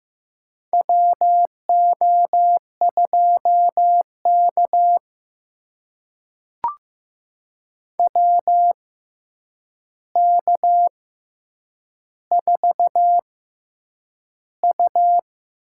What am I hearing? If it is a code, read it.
Morse code, heard as a keyed tone, about 15 wpm, spaced as for 8 wpm, 700 Hz. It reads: WO2K WK4U
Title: USA Call Signs - 6x Character Spacing - 15wpm